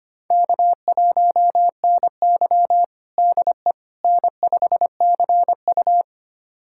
Morse code, heard as a keyed tone, about 25 wpm, 700 Hz. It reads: K1NY DE N5CU
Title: USA Callsign DE Callsign - 4 Characters - 25wpm